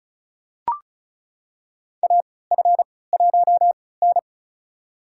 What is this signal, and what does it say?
Morse code, heard as a keyed tone, about 36 wpm, spaced as for 20 wpm, 700 Hz. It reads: AF1N
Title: USA Call Signs - 3x Character Spacing - 35wpm